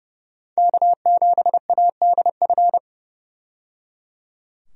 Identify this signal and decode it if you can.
Morse code, heard as a keyed tone, about 31 wpm, 700 Hz. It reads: K7ADF
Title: USA Call Signs - 5 Characters - 30wpm